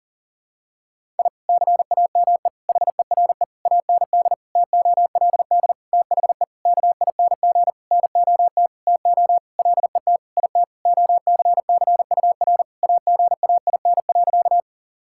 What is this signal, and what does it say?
Morse code, heard as a keyed tone, about 37 wpm, 700 Hz. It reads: I CAME HERE AND TOLD THE KING NOT TO LET IT OCCUR AGAIN.
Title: Sentences from Top 1000 Words - Review Part 6 - 40wpm